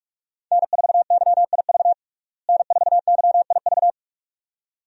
Morse code, heard as a keyed tone, about 46 wpm, 700 Hz. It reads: N4YIV N4YIV
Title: USA Call Signs - 5 Characters - Repeated 2x - 45wpm